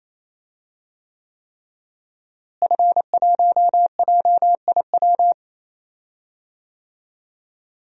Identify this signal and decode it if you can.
Morse code, heard as a keyed tone, about 27 wpm, 700 Hz. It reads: F1JIW
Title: DXCC-100 Call Signs - 5 Characters - Repeated 3x - 28wpm